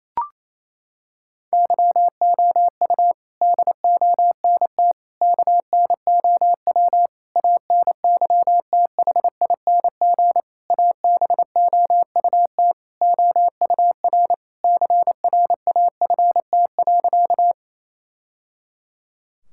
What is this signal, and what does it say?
Morse code, heard as a keyed tone, about 27 wpm, 700 Hz. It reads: YOU DONT KNOW ANYTHING ABOUT OUR CRAFT.
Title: Sentences from Top 1000 Words - Review Part 6 - 28wpm